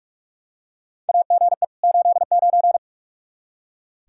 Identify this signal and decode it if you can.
Morse code, heard as a keyed tone, about 46 wpm, 700 Hz. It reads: AGE 89
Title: K1USN SST QSO Practice 45wpm